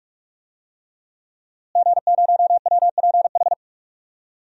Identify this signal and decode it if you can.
Morse code, heard as a keyed tone, about 41 wpm, 700 Hz. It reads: G0WPH
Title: DXCC-100 Call Signs - 5 Characters - Repeated 3x - 45wpm